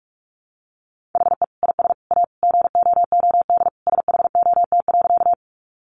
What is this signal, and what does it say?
Morse code, heard as a keyed tone, about 44 wpm, 700 Hz. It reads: HE IS A GOOD SHOT.
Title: Sentences from Top 1100 Words 45wpm